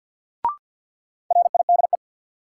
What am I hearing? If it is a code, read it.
Morse code, heard as a keyed tone, about 52 wpm, 700 Hz. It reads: RIDE
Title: Sentences from Top 1200 Words - New Word Drill Alt - 50wpm